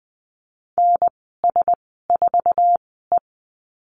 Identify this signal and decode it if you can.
Morse code, heard as a keyed tone, about 20 wpm, spaced as for 15 wpm, 700 Hz. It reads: NS4E
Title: 062 Lesson 11g - Letter h - Call Signs 20wpm